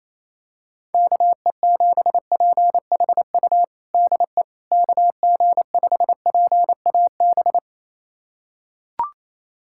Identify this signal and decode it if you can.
Morse code, heard as a keyed tone, about 27 wpm, 700 Hz. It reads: KE7PHU DE KG5PAB
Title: USA Callsign DE Callsign - 6 Characters - 28wpm